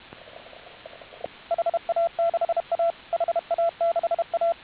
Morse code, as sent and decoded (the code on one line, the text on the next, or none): .... .- -.... .- / .... .- -.... .-
HA6A HA6A